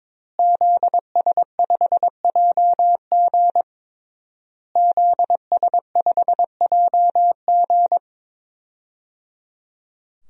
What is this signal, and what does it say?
Morse code, heard as a keyed tone, about 22 wpm, 700 Hz. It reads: ZS5JG ZS5JG
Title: DXCC-100 Call Signs - 5 Characters - Repeated 3x - 22wpm